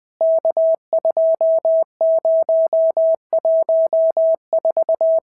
Morse code, heard as a keyed tone, about 19 wpm, 650 Hz. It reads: K2014
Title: POTA QSO Practice - 20wpm